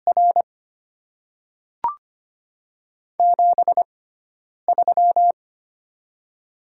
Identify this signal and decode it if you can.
Morse code, heard as a keyed tone, about 25 wpm, spaced as for 9 wpm, 700 Hz. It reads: R 73
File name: Top 20 QSO Elements - 6x Character Spacing - 25wpm